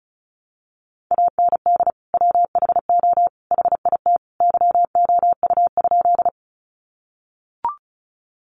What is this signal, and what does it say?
Morse code, heard as a keyed tone, about 34 wpm, 700 Hz. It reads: AND WHO HIT YOU?
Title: Sentences from Top 1200 Words 35wpm